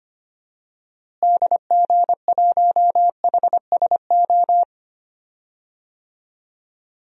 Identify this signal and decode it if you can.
Morse code, heard as a keyed tone, about 25 wpm, 700 Hz. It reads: DG1HSO
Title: DXCC-100 Call Signs - 6 Characters - Repeated 2x - 25wpm